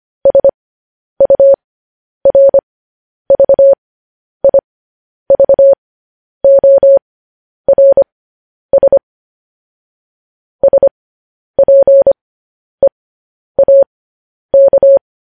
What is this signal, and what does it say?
Morse code, heard as a keyed tone, about 25 wpm, spaced as for 10 wpm, 550 Hz. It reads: SURVIVORS SPEAK
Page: News Headlines in Morse Code at 10 WPM